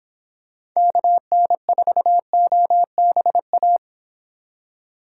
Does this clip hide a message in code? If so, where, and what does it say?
Yes: Morse code from 0.8 to 3.8 s, heard as a keyed tone, about 26 wpm, 700 Hz. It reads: KN4OBA